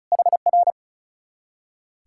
Morse code, heard as a keyed tone, about 41 wpm, 700 Hz. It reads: HR
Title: Speed-Racing Common QSO Phrases - 35wpm